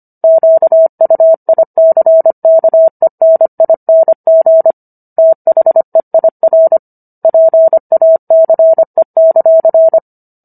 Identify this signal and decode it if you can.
Morse code, heard as a keyed tone, about 25 wpm, 650 Hz. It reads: QUICKENING THEIR PACE;